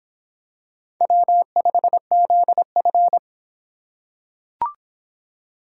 Morse code, heard as a keyed tone, about 25 wpm, 700 Hz. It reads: W5ZF